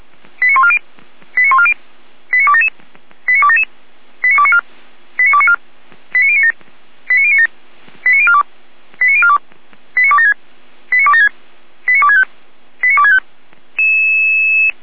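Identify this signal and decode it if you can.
Five-tone selective call (ZVEI), heard as quick runs of five tones, tones of about 70 ms. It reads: 89140 89140 89277 89277 89294 89294 89097 89097 89042 89042 89276 89276 89265 89265
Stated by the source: ZVEI five-tone call 89187 received by SDR.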